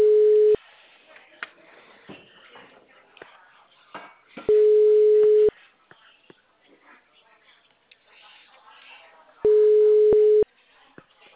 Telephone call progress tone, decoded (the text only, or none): ringback tone